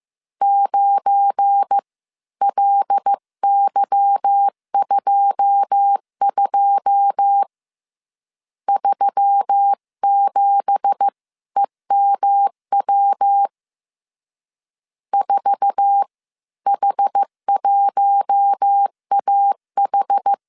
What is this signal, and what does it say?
Morse code, heard as a keyed tone, about 15 wpm, spaced as for 13 wpm, 800 Hz. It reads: 9LY22 37EMW 4H1AH
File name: CW / DET / 1 9 1500C S C 14wpm 3245